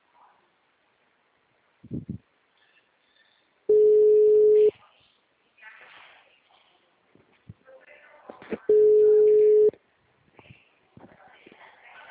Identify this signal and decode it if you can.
Ringback tone (Europe, 425 Hz), with speech.